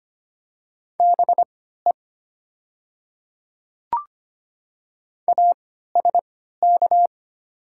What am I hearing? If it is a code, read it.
Morse code, heard as a keyed tone, about 25 wpm, spaced as for 14 wpm, 700 Hz. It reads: BE ASK
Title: 3x Character Spacing - Top 200 Words - 25wpm